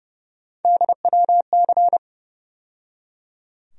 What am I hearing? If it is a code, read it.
Morse code, heard as a keyed tone, about 30 wpm, 700 Hz. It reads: DWC
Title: USA Call Sign Postfixes - 3 Letters 30wpm